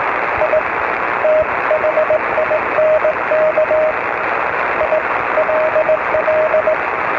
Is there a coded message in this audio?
Yes: Morse code from 0.4 to 6.7 s, heard as a keyed tone, about 18 wpm, 620 Hz, with radio noise. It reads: I THINK ILL